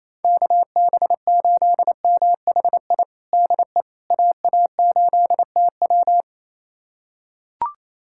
Morse code, heard as a keyed tone, about 28 wpm, 700 Hz. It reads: KB8MHI DE AA8TW
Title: USA Callsign DE Callsign 28wpm